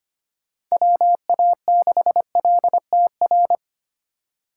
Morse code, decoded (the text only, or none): WA6LTR